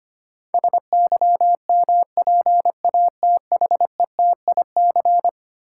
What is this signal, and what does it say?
Morse code, heard as a keyed tone, about 25 wpm, 700 Hz. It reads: SYMPATHETIC
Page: morse-code-ninja-practice-25wpm